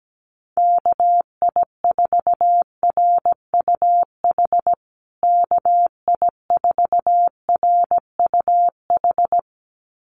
Morse code, heard as a keyed tone, about 17 wpm, 700 Hz. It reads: KI4RUH KI4RUH
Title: USA Call Signs Repeated 2x 17wpm